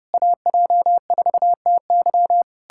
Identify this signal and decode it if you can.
Morse code, heard as a keyed tone, about 33 wpm, 700 Hz. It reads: AJ4TY